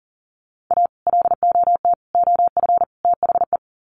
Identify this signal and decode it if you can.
Morse code, heard as a keyed tone, about 37 wpm, 700 Hz. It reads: A LOT OF THE
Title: Sets of 4 Words 40wpm